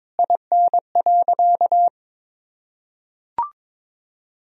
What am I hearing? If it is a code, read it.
Morse code, heard as a keyed tone, about 22 wpm, 700 Hz. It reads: IN.
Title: Sentences from Top 1000 Words - Review Part 2 - 22wpm